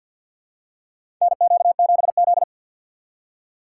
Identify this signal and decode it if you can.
Morse code, heard as a keyed tone, about 49 wpm, 700 Hz. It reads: NQ7Z
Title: USA Call Signs - 6x Character Spacing - 50wpm